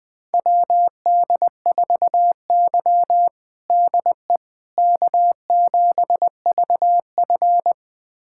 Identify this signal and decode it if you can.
Morse code, heard as a keyed tone, about 20 wpm, 700 Hz. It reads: WD4Y DE K7VF
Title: USA Callsign DE Callsign - 4 Characters - 20wpm